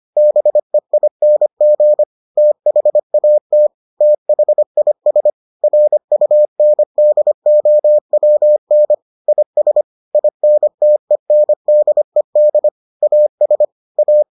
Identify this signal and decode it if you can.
Morse code, heard as a keyed tone, about 25 wpm, 600 Hz. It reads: BEING THAT THIS RUNDOWN IS INTENDED AS A